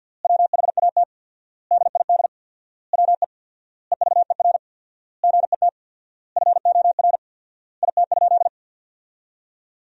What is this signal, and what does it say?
Morse code, heard as a keyed tone, about 46 wpm, 700 Hz. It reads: WHAT DID WE EVER GET FOR IT?
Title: 4x Word-Spacing - Sentences from Top 400 Words - 50wpm